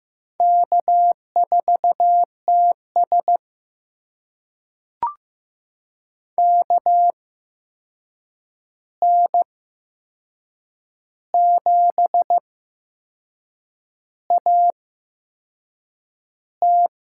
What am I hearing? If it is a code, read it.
Morse code, heard as a keyed tone, about 15 wpm, spaced as for 6 wpm, 700 Hz. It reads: K4TS KN7AT
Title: USA Call Signs - 8x Character Spacing - 15wpm